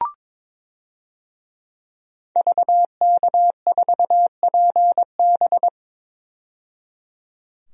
Morse code, heard as a keyed tone, about 22 wpm, 700 Hz. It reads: VK4PB
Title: DXCC-100 Call Signs - 5 Characters - 22wpm